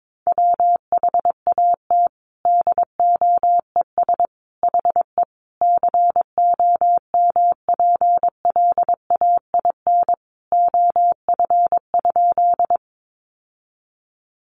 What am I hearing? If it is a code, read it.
Morse code, heard as a keyed tone, about 22 wpm, 700 Hz. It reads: WHAT DOES HE COMPLAIN OF?
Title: Sentences from Top 1300 Words 22wpm